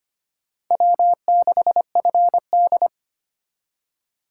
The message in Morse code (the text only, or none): W6FD